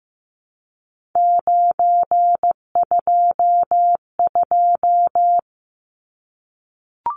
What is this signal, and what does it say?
Morse code, heard as a keyed tone, about 15 wpm, 700 Hz. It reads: 922
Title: Numbers - 3-Digit 15wpm